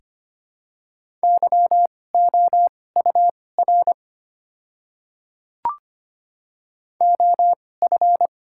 Morse code, heard as a keyed tone, about 25 wpm, spaced as for 18 wpm, 700 Hz. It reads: YOUR OF
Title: 2x Character Spacing - Top 100 Words - 25wpm